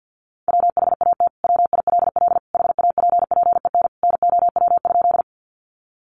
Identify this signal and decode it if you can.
Morse code, heard as a keyed tone, about 50 wpm, 700 Hz. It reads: WHAT WILL HAPPEN NOW?
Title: Sentences from Top 400 Words 50wpm